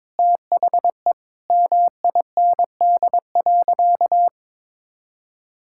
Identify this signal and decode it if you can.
Morse code, heard as a keyed tone, about 21 wpm, 700 Hz. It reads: THE MIND.
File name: practice / 22wpm / Sentences with 4-Word Phrases - 22wpm